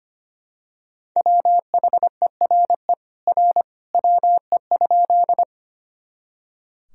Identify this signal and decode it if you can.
Morse code, heard as a keyed tone, about 25 wpm, 700 Hz. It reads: WHERE R WE?
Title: Sentences from Top 200 Words and Abbreviations 25wpm